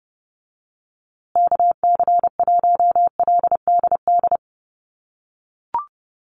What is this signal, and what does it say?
Morse code, heard as a keyed tone, about 30 wpm, 700 Hz. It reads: KC1LDD
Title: USA Call Signs Repeated 2x 30wpm